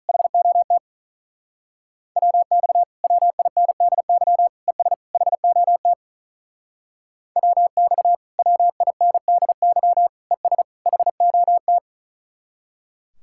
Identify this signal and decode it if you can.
Morse code, heard as a keyed tone, about 44 wpm, 700 Hz. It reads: HOT WX WINDY ES HOT WX WINDY ES HOT